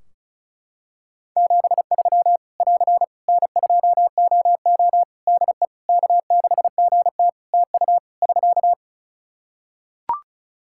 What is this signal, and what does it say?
Morse code, heard as a keyed tone, about 34 wpm, 700 Hz. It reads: 73 <AR> N2OO DE K6GT TU <SK>
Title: Practice QSO - Exchange 4 - Ending - 35wpm